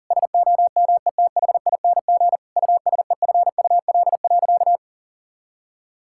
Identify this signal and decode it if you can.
Morse code, heard as a keyed tone, about 37 wpm, 700 Hz. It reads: SOMETHING USEFUL.
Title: Sentences from Top 1000 Words - Review Part 5 - 40wpm